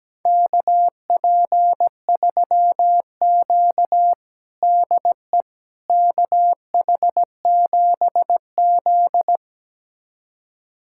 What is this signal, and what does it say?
Morse code, heard as a keyed tone, about 17 wpm, 700 Hz. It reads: KP3Q DE KH7Z
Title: USA Callsign DE Callsign - 4 Characters - 17wpm